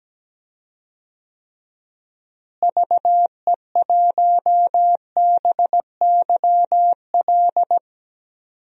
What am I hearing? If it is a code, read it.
Morse code, heard as a keyed tone, about 17 wpm, 700 Hz. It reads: VE1BYL
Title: DXCC-100 Call Signs - 6 Characters - Repeated 3x - 17wpm